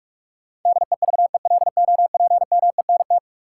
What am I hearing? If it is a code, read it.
Morse code, heard as a keyed tone, about 46 wpm, 700 Hz. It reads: DEVELOPMENT
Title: Sentences from Top 1400 Words - New Word Drill Alt - 45wpm